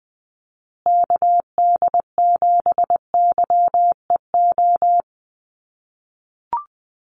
Morse code, heard as a keyed tone, about 20 wpm, 700 Hz. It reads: KD7YEO
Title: USA Call Signs Repeated 2x 20wpm